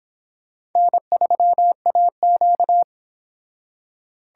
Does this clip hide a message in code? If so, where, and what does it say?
Yes: Morse code from 0.8 to 2.8 s, heard as a keyed tone, about 24 wpm, 700 Hz. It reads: N3AQ